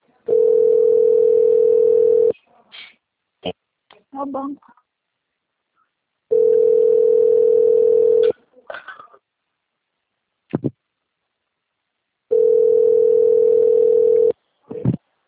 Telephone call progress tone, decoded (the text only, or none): ringback tone